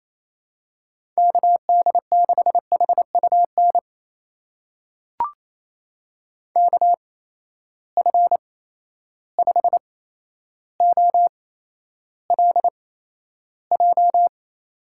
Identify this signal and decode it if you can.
Morse code, heard as a keyed tone, about 28 wpm, spaced as for 12 wpm, 700 Hz. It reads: KD6HUN KF5OLJ
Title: USA Call Signs - 6 Characters - 8x Character Spacing - 28wpm